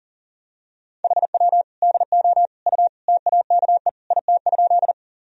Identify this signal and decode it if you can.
Morse code, heard as a keyed tone, about 40 wpm, 700 Hz. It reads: HW DO U TAKE IT?
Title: Sentences from Top 200 Words and Abbreviations 40wpm